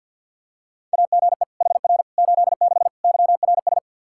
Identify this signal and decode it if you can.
Morse code, heard as a keyed tone, about 52 wpm, 700 Hz. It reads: AGE HR 86 YRS
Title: Practice QSO - Exchange 3 - More Info - 50wpm